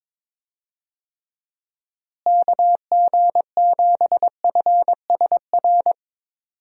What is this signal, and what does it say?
Morse code, heard as a keyed tone, about 22 wpm, 700 Hz. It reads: KG7FSR
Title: USA Call Signs - 6 Characters - Repeated 2x - 22wpm